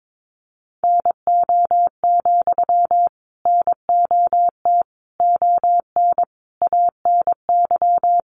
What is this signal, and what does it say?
Morse code, heard as a keyed tone, about 22 wpm, 700 Hz. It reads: NO, NOT ON ANY
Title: Sentences from Top 900 Words 22wpm